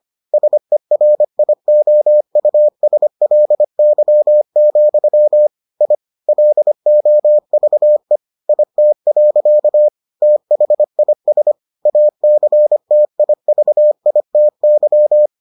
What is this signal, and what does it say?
Morse code, heard as a keyed tone, about 25 wpm, 600 Hz. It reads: SERIOUSLY, I LOVE IT. THIS ACTIVITY